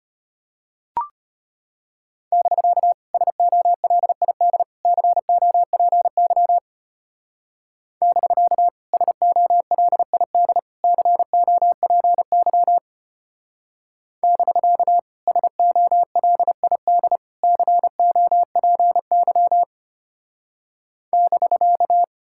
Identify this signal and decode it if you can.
Morse code, heard as a keyed tone, about 30 wpm, 700 Hz. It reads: <BK> SOLID COPY <BK> SOLID COPY <BK> SOLID COPY <BK>
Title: Speed-Racing Common QSO Phrases Encore - 25wpm